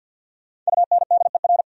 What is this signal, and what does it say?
Morse code, heard as a keyed tone, about 66 wpm, spaced as for 58 wpm, 700 Hz. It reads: UNDER
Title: Speed-Racing Top 200 Words 50wpm